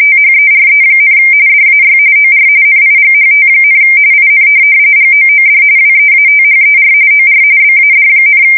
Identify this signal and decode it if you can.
RTTY, about 45.45 baud. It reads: THERE, DAY OR NIGHT, IN GOOD WEATHER OR BAD,
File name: fsk170 / USS-New-Jersey